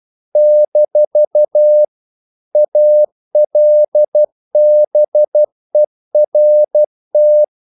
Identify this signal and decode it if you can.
Morse code, heard as a keyed tone, about 12 wpm, 600 Hz. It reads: - ALBERT
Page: English quotations in CW (telegraphy - morse)